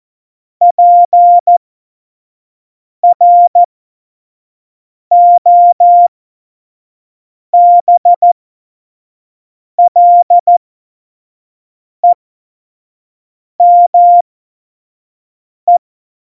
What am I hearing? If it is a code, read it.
Morse code, heard as a keyed tone, about 14 wpm, spaced as for 5 wpm, 700 Hz. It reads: PROBLEME